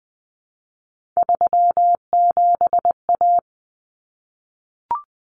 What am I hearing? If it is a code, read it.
Morse code, heard as a keyed tone, about 20 wpm, 700 Hz. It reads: 37A
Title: Common Cut Numbers - 3-Digit 20wpm